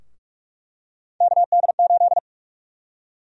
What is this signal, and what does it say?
Morse code, heard as a keyed tone, about 46 wpm, 700 Hz. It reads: KD8
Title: USA Call Sign Prefixes - 45wpm